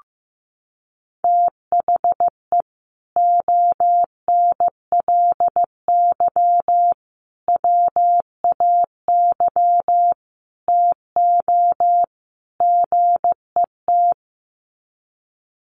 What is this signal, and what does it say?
Morse code, heard as a keyed tone, about 15 wpm, 700 Hz. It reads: THE ONLY WAY TO GET
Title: Sets of 5 Words Encore2 15wpm